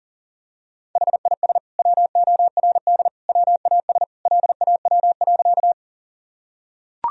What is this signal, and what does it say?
Morse code, heard as a keyed tone, about 40 wpm, 700 Hz. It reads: HIS WORD WAS LAW.
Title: Sentences from Top 1000 Words - Review Part 3 - 40wpm